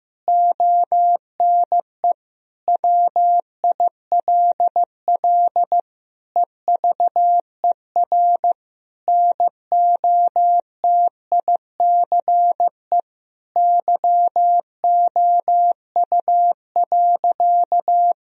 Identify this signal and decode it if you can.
Morse code, heard as a keyed tone, about 15 wpm, 700 Hz. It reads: ONE WILL EVER NOTICE YOU.
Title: Sentences from Top 1000 Words - Review Part 4 - 15wpm